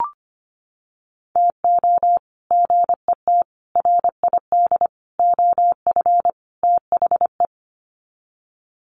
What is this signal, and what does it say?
Morse code, heard as a keyed tone, about 25 wpm, 700 Hz. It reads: TO GET RID OF THE
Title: Sets of 5 Words 25wpm